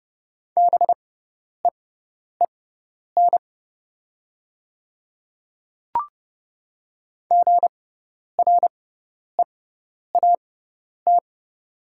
Morse code, heard as a keyed tone, about 30 wpm, spaced as for 10 wpm, 700 Hz. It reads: BEEN GREAT
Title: 6x Character Spacing - Top 200 Words - 30wpm